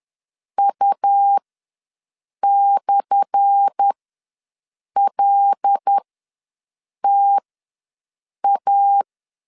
Morse code, heard as a keyed tone, about 10 wpm, spaced as for 6 wpm, 800 Hz. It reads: U/LTA
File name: LEZIONI / 10 SIGNS / 10A SLASH ws 10wpm 4121